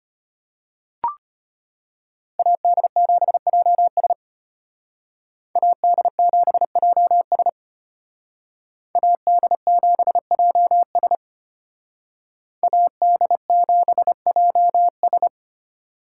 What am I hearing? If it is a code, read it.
Morse code, heard as a keyed tone, about 31 wpm, 700 Hz. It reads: AD7JS AD7JS AD7JS AD7JS